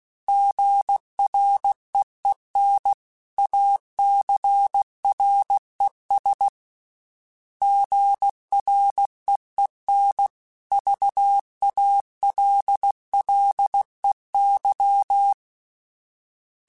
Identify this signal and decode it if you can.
Morse code, heard as a keyed tone, about 16 wpm, 800 Hz. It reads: GREEN ACRES GREEN VALLEY
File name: fromSMs / FromSMs mp3s / Cities@15